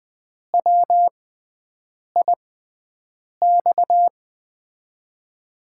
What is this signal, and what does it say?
Morse code, heard as a keyed tone, about 20 wpm, spaced as for 7 wpm, 700 Hz. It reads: WIX
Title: USA Call Signs - 6x Character Spacing - 20wpm